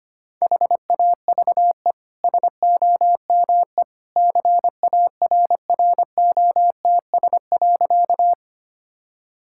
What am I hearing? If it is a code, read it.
Morse code, heard as a keyed tone, about 25 wpm, 690 Hz. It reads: HAVE SOME CARROTS.
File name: Easy Sentences with Hints - 25wpm